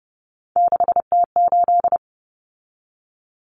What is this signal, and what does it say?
Morse code, heard as a keyed tone, about 30 wpm, 700 Hz. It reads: 6T8
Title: Common Cut Numbers - 3-Digit 30wpm